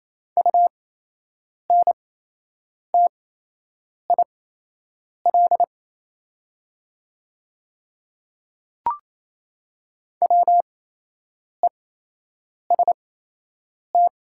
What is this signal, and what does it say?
Morse code, heard as a keyed tone, about 28 wpm, spaced as for 8 wpm, 700 Hz. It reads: UNTIL WEST
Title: 8x Character Spacing - Top 500 Words - 28wpm